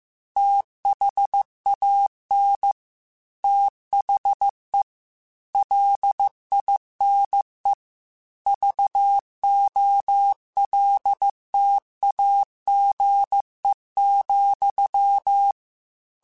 Morse code, heard as a keyed tone, about 15 wpm, 800 Hz. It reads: THAN THE LINE VOLTAGE,
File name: Technical Subjects / Technical Subjects-mp3s / Sw Pwr Sup @13